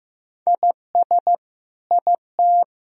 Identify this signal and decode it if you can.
Morse code, heard as a keyed tone, about 15 wpm, 700 Hz. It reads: IS IT
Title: Sentences from Top 1000 Words - Review Part 3 - 15wpm